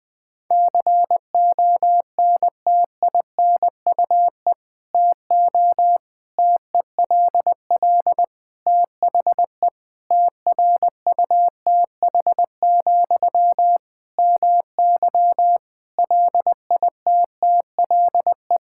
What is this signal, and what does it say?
Morse code, heard as a keyed tone, about 21 wpm, 700 Hz. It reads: CONTINUE TO TELL THE TRUTH, MY LITTLE
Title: Sentences from Top 1000 Words - Review Part 6 - 20wpm